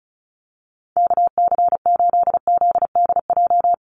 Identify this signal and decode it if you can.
Morse code, heard as a keyed tone, about 34 wpm, 700 Hz. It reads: KC8ZDJ
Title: USA Call Signs 35wpm